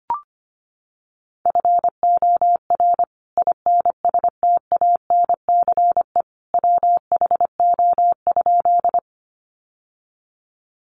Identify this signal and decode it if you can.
Morse code, heard as a keyed tone, about 25 wpm, 700 Hz. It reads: FOR INSTANCE WHO?
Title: Sentences from Top 800 Words 25wpm